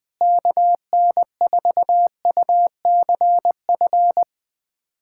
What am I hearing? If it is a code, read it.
Morse code, heard as a keyed tone, about 21 wpm, 700 Hz. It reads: KN4UCF
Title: USA Callsign DE Callsign 20wpm